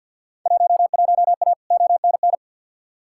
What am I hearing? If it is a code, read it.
Morse code, heard as a keyed tone, about 47 wpm, 700 Hz. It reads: 11A ONN
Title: Field Day Exchanges Encore - 50wpm